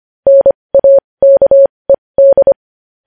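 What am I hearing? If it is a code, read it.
Morse code, heard as a keyed tone, about 25 wpm, spaced as for 20 wpm, 550 Hz. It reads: NAKED
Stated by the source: Morse code news headlines practice at 20.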